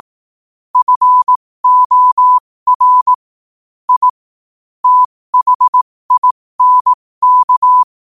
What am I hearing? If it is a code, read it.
Morse code, heard as a keyed tone, about 18 wpm, spaced as for 15 wpm, 1000 Hz. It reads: FOR I THINK
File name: qotdfiles / 15wpm